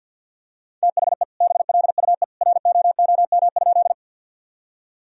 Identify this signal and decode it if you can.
Morse code, heard as a keyed tone, about 50 wpm, 700 Hz. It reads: THE BLUE ROOM?
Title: Sentences from Top 1000 Words - Review Part 3 - 50wpm